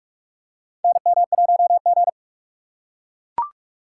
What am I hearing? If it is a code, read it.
Morse code, heard as a keyed tone, about 44 wpm, 700 Hz. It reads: NM1G
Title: USA Call Signs - 4 Characters - 6x Character Spacing - 45wpm